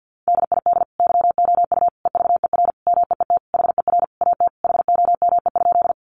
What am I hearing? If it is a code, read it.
Morse code, heard as a keyed tone, about 50 wpm, 700 Hz. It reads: DID YOU EVER MEET HER AT HOME?
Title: Sentences from Top 1100 Words 50wpm